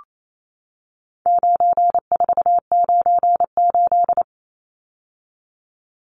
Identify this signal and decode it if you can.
Morse code, heard as a keyed tone, about 28 wpm, 700 Hz. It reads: 9498
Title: Numbers - 4-Digit 28wpm